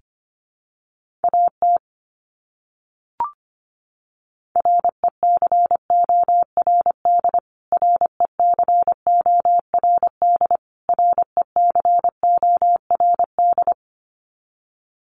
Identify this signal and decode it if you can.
Morse code, heard as a keyed tone, about 25 wpm, 700 Hz. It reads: AT RECORD RECORD RECORD
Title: Top 500 Words Repeated 25wpm